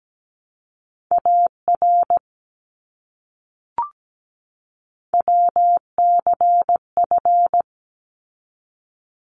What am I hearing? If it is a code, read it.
Morse code, heard as a keyed tone, about 17 wpm, 700 Hz. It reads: AR WCF
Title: ARRL Sections 17wpm